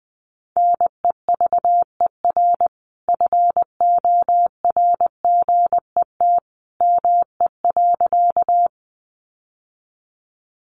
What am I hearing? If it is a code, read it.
Morse code, heard as a keyed tone, about 20 wpm, 700 Hz. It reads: NEVER FORGET ME.
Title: Sentences from Top 1200 Words 20wpm